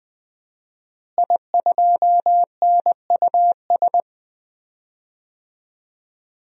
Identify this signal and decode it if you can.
Morse code, heard as a keyed tone, about 21 wpm, 700 Hz. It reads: I2NUS
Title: DXCC-100 Call Signs - 5 Characters - Repeated 2x - 20wpm